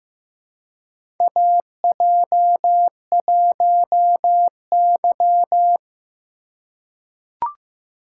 Morse code, heard as a keyed tone, about 15 wpm, 700 Hz. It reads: AJ1Y
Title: USA Call Signs - 4 Characters - 4x Character Spacing - 15wpm